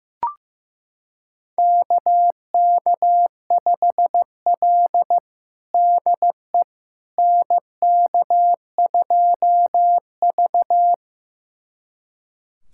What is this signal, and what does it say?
Morse code, heard as a keyed tone, about 15 wpm, 700 Hz. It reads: KK5L DE NK2V
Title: USA Callsign DE Callsign - 4 Characters - 15wpm